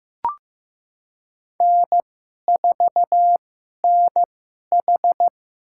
Morse code, heard as a keyed tone, about 15 wpm, spaced as for 11 wpm, 700 Hz. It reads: N4NH
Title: USA Call Signs - 4 Characters - 2x Character Spacing - 15wpm